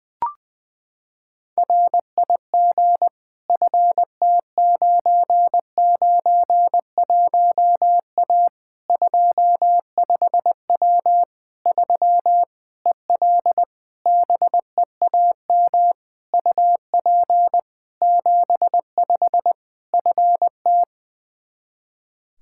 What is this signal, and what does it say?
Morse code, heard as a keyed tone, about 21 wpm, 700 Hz. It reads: RIG FT991A 25W 3 EL BEAM UP 75 FT
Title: K1USN SST QSO Practice 20wpm